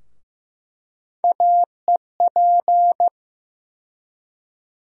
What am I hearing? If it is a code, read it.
Morse code, heard as a keyed tone, about 15 wpm, 700 Hz. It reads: AEP
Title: USA Call Sign Postfixes - 3 Letters 15wpm